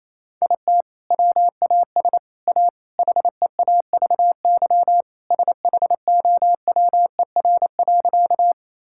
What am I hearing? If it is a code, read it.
Morse code, heard as a keyed tone, about 28 wpm, 700 Hz. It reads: IT WAS A HEAVY SHOWER.
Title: Sentences from Top 1600 Words 28wpm